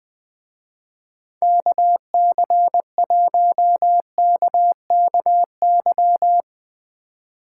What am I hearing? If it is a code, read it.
Morse code, heard as a keyed tone, about 20 wpm, 700 Hz. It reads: KC1KKY
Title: Speed Racing - USA 2x3 Call Signs 17wpm